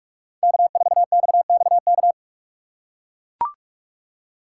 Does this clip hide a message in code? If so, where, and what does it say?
Yes: Morse code from 0.4 to 2.1 s, heard as a keyed tone, about 44 wpm, 700 Hz. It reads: K4XXK